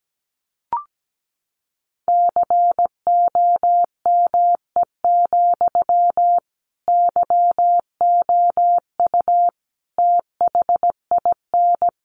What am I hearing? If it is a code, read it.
Morse code, heard as a keyed tone, about 17 wpm, 700 Hz. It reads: COME, YOU THIN
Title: Sentences from Top 1200 Words 17wpm